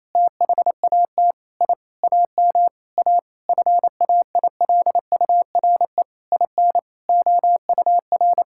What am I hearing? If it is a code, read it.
Morse code, heard as a keyed tone, about 28 wpm, 700 Hz. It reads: THAT I AM A FAILURE IN OUR
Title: Sentences from Top 1000 Words - Review Part 6 - 28wpm